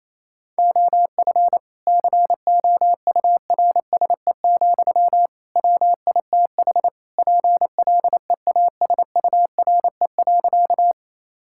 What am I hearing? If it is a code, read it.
Morse code, heard as a keyed tone, about 28 wpm, 700 Hz. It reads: OF COURSE, WITH PLEASURE.
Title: Sentences from Top 1600 Words 28wpm